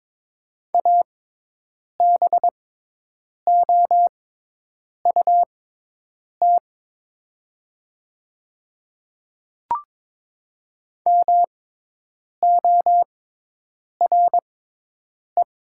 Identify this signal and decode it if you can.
Morse code, heard as a keyed tone, about 21 wpm, spaced as for 8 wpm, 700 Hz. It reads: ABOUT MORE
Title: 6x Character Spacing - Top 100 Words - 22wpm